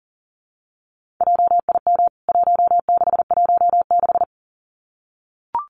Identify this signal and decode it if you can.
Morse code, heard as a keyed tone, about 40 wpm, 700 Hz. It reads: JIM 1616
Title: CWT Exchange 40wpm